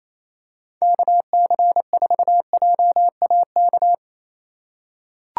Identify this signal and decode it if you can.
Morse code, heard as a keyed tone, about 28 wpm, 700 Hz. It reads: KC4JAK
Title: USA Call Signs - 6 Characters - 4x Character Spacing - 28wpm